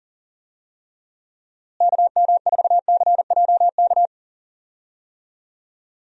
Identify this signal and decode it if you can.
Morse code, heard as a keyed tone, about 40 wpm, 700 Hz. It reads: KM4CJK
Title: USA Call Signs - 6 Characters - Repeated 3x - 40wpm